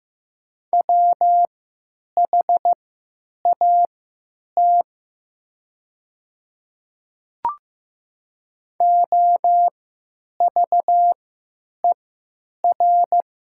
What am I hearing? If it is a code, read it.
Morse code, heard as a keyed tone, about 15 wpm, spaced as for 9 wpm, 700 Hz. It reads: WHAT OVER
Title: 3x Character Spacing - Top 100 Words - 15wpm